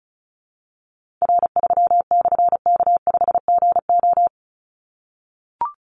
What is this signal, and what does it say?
Morse code, heard as a keyed tone, about 36 wpm, 700 Hz. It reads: R3/K5GO
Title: Call Signs - 35wpm